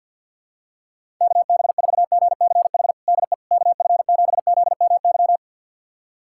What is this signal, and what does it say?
Morse code, heard as a keyed tone, about 50 wpm, 700 Hz. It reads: KB4GCH DE KF7ZMY
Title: USA Callsign DE Callsign 50wpm